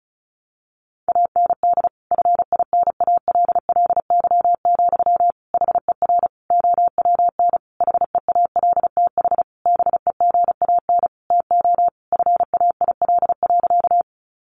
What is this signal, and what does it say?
Morse code, heard as a keyed tone, about 34 wpm, 700 Hz. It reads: AND FINALLY, HER OWN HEALTH BEGAN TO FAIL.
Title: Sentences from Top 1100 Words 35wpm